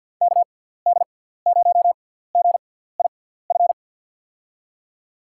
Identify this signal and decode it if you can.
Morse code, heard as a keyed tone, about 54 wpm, spaced as for 17 wpm, 700 Hz. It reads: KD0GIF
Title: USA Call Signs - 6 Characters - 6x Character Spacing - 50wpm